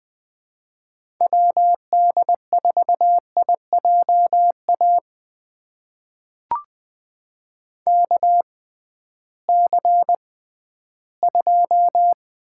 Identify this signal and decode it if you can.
Morse code, heard as a keyed tone, about 20 wpm, spaced as for 13 wpm, 700 Hz. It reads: WD4IJA KC2
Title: USA Call Signs - 6 Characters - 6x Character Spacing - 20wpm